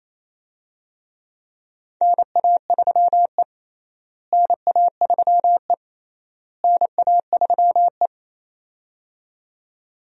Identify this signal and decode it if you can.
Morse code, heard as a keyed tone, about 27 wpm, 700 Hz. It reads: NA3E NA3E NA3E
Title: USA Call Signs - 4 Characters - Repeated 3x - 28wpm